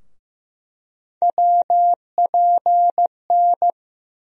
Morse code, heard as a keyed tone, about 15 wpm, 700 Hz. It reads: WPN
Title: USA Call Sign Postfixes - 3 Letters 15wpm